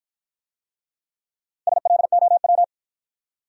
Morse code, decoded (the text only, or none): SLOW